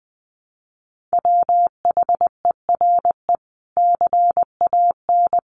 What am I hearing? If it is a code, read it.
Morse code, heard as a keyed tone, about 20 wpm, 700 Hz. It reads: WHERE CAN
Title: Sentences from Top 500 Words 20wpm